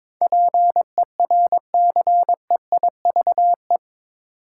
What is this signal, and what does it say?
Morse code, heard as a keyed tone, about 21 wpm, 700 Hz. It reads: PERCEIVE
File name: practice / 22wpm / Eight Letter Words - 22wpm